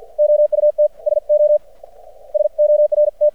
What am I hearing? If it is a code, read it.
Morse code, heard as a keyed tone, about 53 wpm, spaced as for 46 wpm, 600 Hz. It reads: OUT SO SORT
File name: archived / CRSnet